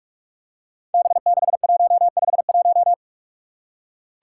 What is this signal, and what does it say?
Morse code, heard as a keyed tone, about 46 wpm, 700 Hz. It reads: B6151
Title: Five Challenging Letter-Number Combinations 45wpm